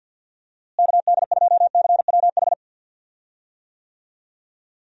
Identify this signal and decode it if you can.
Morse code, heard as a keyed tone, about 51 wpm, 700 Hz. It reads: KD2CWH
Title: USA Call Signs - 6 Characters - Repeated 3x - 50wpm